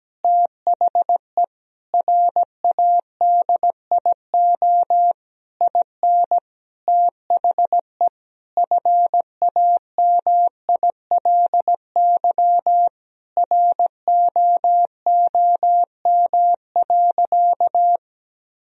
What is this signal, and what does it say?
Morse code, heard as a keyed tone, about 17 wpm, 700 Hz. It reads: THE RADIO IN THE FAMILY ROOM.
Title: Sentences from Top 1000 Words - Review Part 3 - 17wpm